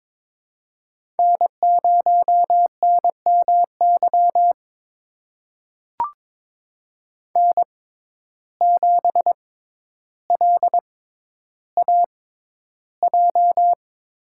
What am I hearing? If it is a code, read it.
Morse code, heard as a keyed tone, about 22 wpm, spaced as for 11 wpm, 700 Hz. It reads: N0NMY N7LAJ
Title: USA Call Signs - 5 Characters - 6x Character Spacing - 22wpm